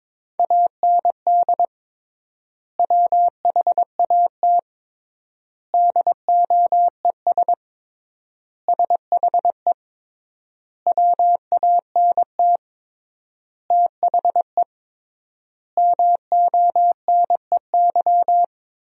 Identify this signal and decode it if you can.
Morse code, heard as a keyed tone, about 23 wpm, 700 Hz. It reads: AND WHAT DOES SHE WANT THE MONEY
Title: 3x Word-Spacing - Sentences from Top 400 Words - 22wpm